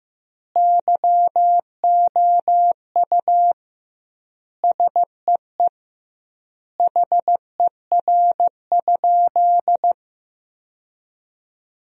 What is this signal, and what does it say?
Morse code, heard as a keyed tone, about 16 wpm, 700 Hz. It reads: YOU SEE HER?
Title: 2x Word-Spacing - Sentences from Top 100 Words - 15wpm